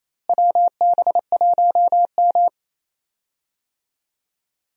Morse code, heard as a keyed tone, about 28 wpm, 700 Hz. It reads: WB1M
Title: USA Call Signs - 4 Characters - Repeated 2x - 28wpm